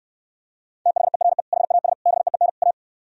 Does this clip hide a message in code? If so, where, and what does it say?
Yes: Morse code from 0.9 to 2.7 s, heard as a keyed tone, about 67 wpm, 700 Hz. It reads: THERE HAS BEEN A